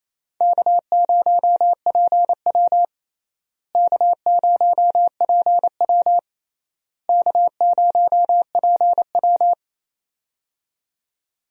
Morse code, heard as a keyed tone, about 28 wpm, 700 Hz. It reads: K0PW K0PW K0PW
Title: USA Call Signs - 4 Characters - Repeated 3x - 28wpm